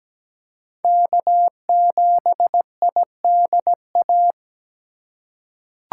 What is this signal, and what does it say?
Morse code, heard as a keyed tone, about 17 wpm, 700 Hz. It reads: K7IDA
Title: USA Call Signs - 5 Characters - 8x Character Spacing - 17wpm